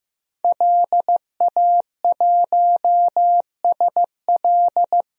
Morse code, heard as a keyed tone, about 15 wpm, 700 Hz. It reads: LA1SL
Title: DXCC-100 Call Signs - 5 Characters - 15wpm